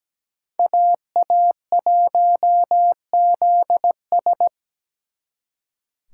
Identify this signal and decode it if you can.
Morse code, heard as a keyed tone, about 20 wpm, 700 Hz. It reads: AA1ZS
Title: Speed Racing - USA 2x2 Call Signs 17wpm